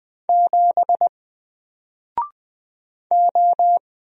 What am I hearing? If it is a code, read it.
Morse code, heard as a keyed tone, about 20 wpm, 700 Hz. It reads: 7 O
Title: Single Letter-Number Alt 20wpm